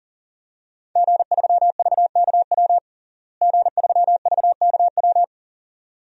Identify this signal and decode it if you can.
Morse code, heard as a keyed tone, about 40 wpm, 690 Hz. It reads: G3VKW G3VKW
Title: DXCC-100 Call Signs - 5 Characters - Repeated 2x - 40wpm